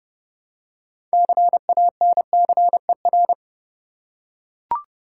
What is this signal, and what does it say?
Morse code, heard as a keyed tone, about 32 wpm, 700 Hz. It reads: CANCER
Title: Six Letter Words - 30wpm